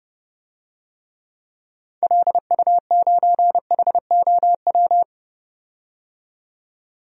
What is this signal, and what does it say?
Morse code, heard as a keyed tone, about 30 wpm, 700 Hz. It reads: LU9HOW
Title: DXCC-100 Call Signs - 6 Characters - 30wpm